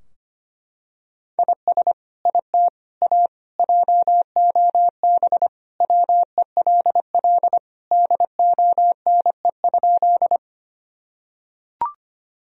Morse code, heard as a keyed tone, about 25 wpm, 700 Hz. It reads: IS IT A JOB WELL DONE?
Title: Sentences from Top 1000 Words - Review Part 3 - 25wpm